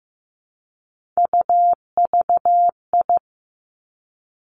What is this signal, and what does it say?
Morse code, heard as a keyed tone, about 15 wpm, 700 Hz. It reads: UVI
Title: Three Challenging Letter-Number Combinations 15wpm